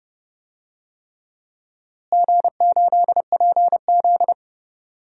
Morse code, heard as a keyed tone, about 30 wpm, 700 Hz. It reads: G8PZ
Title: DXCC-100 Call Signs - 4 Characters - Repeated 3x - 30wpm